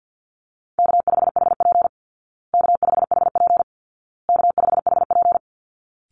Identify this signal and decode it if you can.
Morse code, heard as a keyed tone, about 50 wpm, 700 Hz. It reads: K5HP K5HP K5HP
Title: Call Signs Repeated 2x - 50wpm